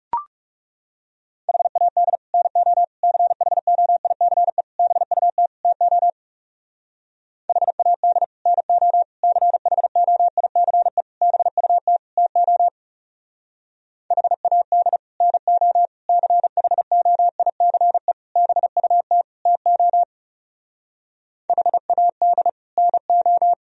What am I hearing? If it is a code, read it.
Morse code, heard as a keyed tone, about 39 wpm, 700 Hz. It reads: HAD NO CHOICE BUT TO HAD NO CHOICE BUT TO HAD NO CHOICE BUT TO HAD NO